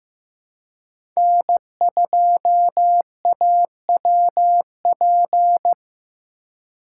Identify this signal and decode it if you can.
Morse code, heard as a keyed tone, about 15 wpm, 700 Hz. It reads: N2AWP
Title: USA Call Signs - 5 Characters - 4x Character Spacing - 15wpm